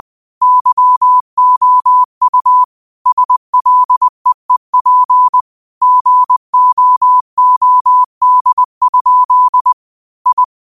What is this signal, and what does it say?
Morse code, heard as a keyed tone, about 20 wpm, 1000 Hz. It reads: YOU SLEEP GOOD? I
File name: qotdfiles / 20wpm